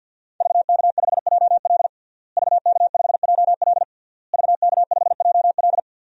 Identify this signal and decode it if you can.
Morse code, heard as a keyed tone, about 53 wpm, 700 Hz. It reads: VK5JL VK5JL VK5JL
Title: DXCC-100 Call Signs - 5 Characters - Repeated 3x - 50wpm